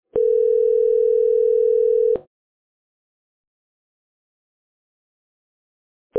Ringback tone (North America), 440 + 480 Hz.